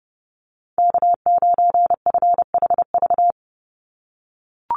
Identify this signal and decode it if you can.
Morse code, heard as a keyed tone, about 30 wpm, 700 Hz. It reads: K9FHV
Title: USA Call Signs 30wpm